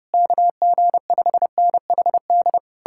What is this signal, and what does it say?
Morse code, heard as a keyed tone, about 33 wpm, 700 Hz. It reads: KG5NHD